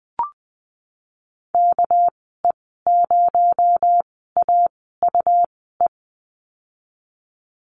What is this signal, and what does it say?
Morse code, heard as a keyed tone, about 20 wpm, 700 Hz. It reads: K E 0 A U E
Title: 238 Lesson 38g - Letter q - Call Signs 20wpm